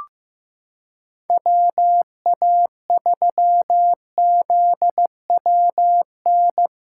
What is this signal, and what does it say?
Morse code, heard as a keyed tone, about 15 wpm, 700 Hz. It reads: WA3ZWN
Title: USA Callsign DE Callsign 15wpm